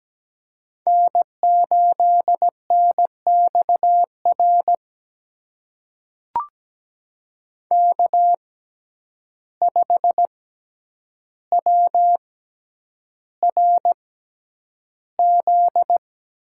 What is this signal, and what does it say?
Morse code, heard as a keyed tone, about 17 wpm, spaced as for 9 wpm, 700 Hz. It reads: N8NXR K5WRZ
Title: USA Call Signs - 5 Characters - 6x Character Spacing - 17wpm